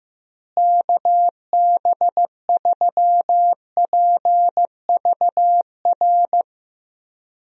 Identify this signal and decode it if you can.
Morse code, heard as a keyed tone, about 16 wpm, 700 Hz. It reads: KB3PVR
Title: USA Call Signs - 6 Characters - Repeated 2x - 15wpm